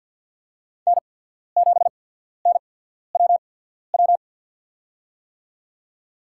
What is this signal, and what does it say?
Morse code, heard as a keyed tone, about 53 wpm, spaced as for 14 wpm, 700 Hz. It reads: N7NWW
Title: USA Call Signs - 5 Characters - 8x Character Spacing - 50wpm